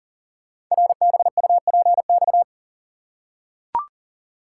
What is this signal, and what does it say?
Morse code, heard as a keyed tone, about 42 wpm, 700 Hz. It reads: RBUPX
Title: Code Groups - 5 Letters - 40wpm